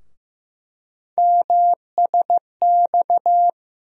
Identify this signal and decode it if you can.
Morse code, heard as a keyed tone, about 15 wpm, 700 Hz. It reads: MSX
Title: USA Call Sign Postfixes - 3 Letters 15wpm